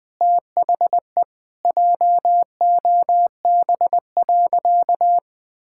Morse code, heard as a keyed tone, about 20 wpm, 700 Hz. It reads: THE JOB.
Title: Sentences from Top 1000 Words - Review Part 5 - 20wpm